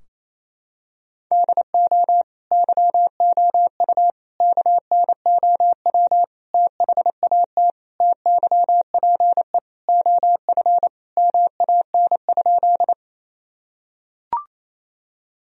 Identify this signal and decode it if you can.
Morse code, heard as a keyed tone, about 28 wpm, 700 Hz. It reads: DO YOU KNOW THAT TYPE OF MAN?
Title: Sentences from Top 1000 Words - Review Part 3 - 28wpm